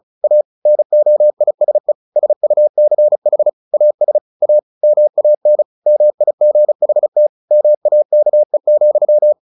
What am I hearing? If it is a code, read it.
Morse code, heard as a keyed tone, about 34 wpm, 600 Hz. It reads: A NOISE SUCH AS A MAN MIGHT MAKE,
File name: the old man and the sea 2 35wpm